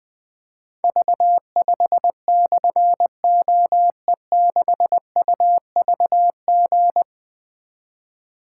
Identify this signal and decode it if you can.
Morse code, heard as a keyed tone, about 20 wpm, 700 Hz. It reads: V5/OE6UVG
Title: Difficult Call Signs 20wpm